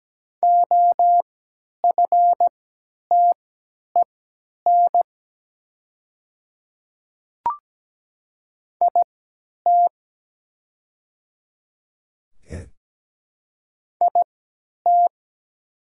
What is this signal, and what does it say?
Morse code, heard as a keyed tone, about 17 wpm, spaced as for 10 wpm, 700 Hz. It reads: OFTEN IT IT
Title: 3x Character Spacing - Top 300 Words - 17wpm